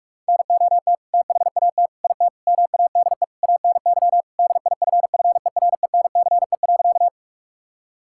Morse code, heard as a keyed tone, about 46 wpm, 700 Hz. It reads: NOT THAT IT MADE ANY DIFFERENCE.
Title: Sentences from Top 1000 Words - Review Part 5 - 45wpm